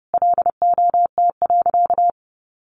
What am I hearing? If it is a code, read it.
Morse code, heard as a keyed tone, about 28 wpm, 700 Hz. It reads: LOT.
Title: Sentences from Top 1500 Words 30wpm